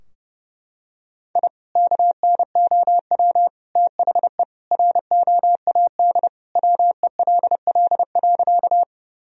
Morse code, heard as a keyed tone, about 30 wpm, 700 Hz. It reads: I KNOW THE ROAD WELL.
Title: Sentences from Top 1000 Words - Review Part 2 - 30wpm